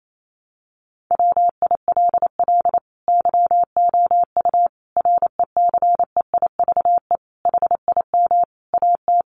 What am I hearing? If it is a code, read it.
Morse code, heard as a keyed tone, about 28 wpm, 700 Hz. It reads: WILL YOU RECEIVE HIM AT
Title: Sentences from Top 1000 Words 28wpm